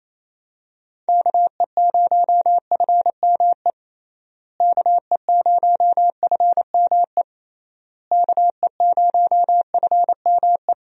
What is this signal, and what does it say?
Morse code, heard as a keyed tone, about 27 wpm, 700 Hz. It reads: KE0FME KE0FME KE0FME
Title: USA Call Signs - 6 Characters - Repeated 3x - 28wpm